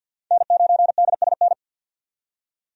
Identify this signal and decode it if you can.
Morse code, heard as a keyed tone, about 52 wpm, 700 Hz. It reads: N9DSN
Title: USA Call Signs - 5 Characters - 50wpm